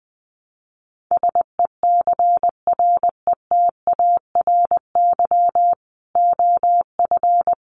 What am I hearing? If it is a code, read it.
Morse code, heard as a keyed tone, about 20 wpm, 700 Hz. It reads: SECRETARY OF
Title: Sentences from Top 1300 Words 20wpm